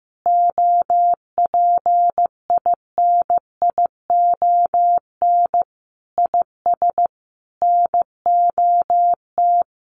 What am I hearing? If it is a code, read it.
Morse code, heard as a keyed tone, about 15 wpm, 700 Hz. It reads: OPINION IS NOT
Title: Sentences from Top 1400 Words 15wpm